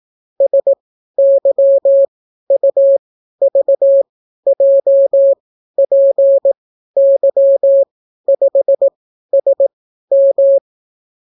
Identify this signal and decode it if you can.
Morse code, heard as a keyed tone, about 18 wpm, spaced as for 12 wpm, 550 Hz. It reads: SYUVJPY5SM